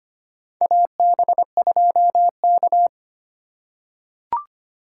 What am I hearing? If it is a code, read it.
Morse code, heard as a keyed tone, about 25 wpm, 700 Hz. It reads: AB2K
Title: USA Call Signs - 4 Characters - 3x Character Spacing - 25wpm